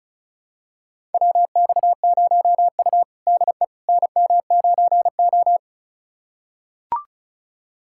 Morse code, heard as a keyed tone, about 34 wpm, 700 Hz. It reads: WX0U DE NM9O
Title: USA Callsign DE Callsign - 4 Characters - 35wpm